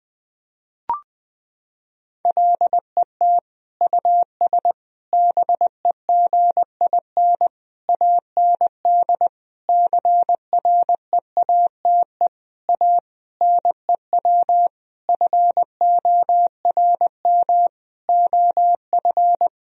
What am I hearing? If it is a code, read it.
Morse code, heard as a keyed tone, about 20 wpm, 700 Hz. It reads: LET US BEGIN AND CREATE A NEW FORM OF
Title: Sentences from Top 1000 Words - Review Part 3 - 20wpm